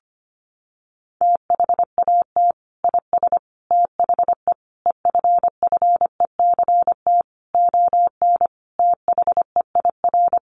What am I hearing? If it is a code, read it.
Morse code, heard as a keyed tone, about 25 wpm, 700 Hz. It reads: THAT IS THE EFFECT ON THEIR
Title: Sentences from Top 1300 Words 25wpm